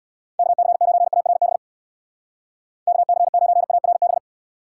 Morse code, heard as a keyed tone, about 60 wpm, 700 Hz. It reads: KC9ARB KC9ARB
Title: Speed Racing - USA Call Signs 50wpm